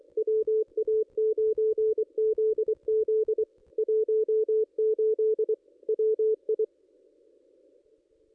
Morse code, heard as a keyed tone, about 23 wpm, 440 Hz. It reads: WA9ZZ 18 WI
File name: contests / ss-cw-k5tr-op-n5rz -1900